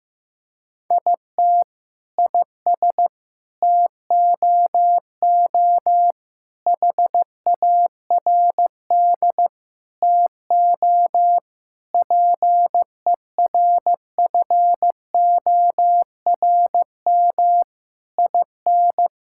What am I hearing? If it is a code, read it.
Morse code, heard as a keyed tone, about 15 wpm, 700 Hz. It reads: IT IS TOO HARD TO PERFORM IN
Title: Sentences from Top 1000 Words - Review Part 5 - 15wpm